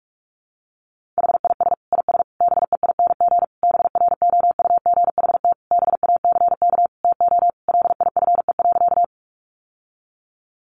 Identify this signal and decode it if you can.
Morse code, heard as a keyed tone, about 44 wpm, 700 Hz. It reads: HIS IS BEING BROUGHT BACK TO LIFE.
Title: Sentences from Top 600 Words 45wpm